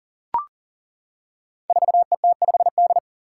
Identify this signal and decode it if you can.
Morse code, heard as a keyed tone, about 38 wpm, 700 Hz. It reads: 4ET5D
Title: Five Challenging Letter-Number Combinations 40wpm